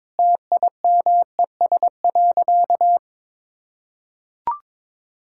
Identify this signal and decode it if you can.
Morse code, heard as a keyed tone, about 22 wpm, 700 Hz. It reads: TIMES.
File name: practice / 22wpm / Sentences with 4-Word Phrases - 22wpm